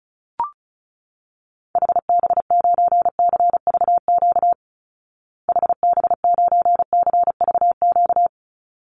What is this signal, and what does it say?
Morse code, heard as a keyed tone, about 34 wpm, 700 Hz. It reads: HB9CVQ HB9CVQ
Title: Call Signs Repeated 3x - 35wpm